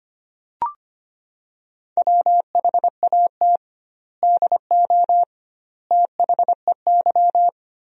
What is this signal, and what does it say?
Morse code, heard as a keyed tone, about 25 wpm, 700 Hz. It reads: WHAT DO THEY
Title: 2x Word-Spacing - Sentences from Top 100 Words - 25wpm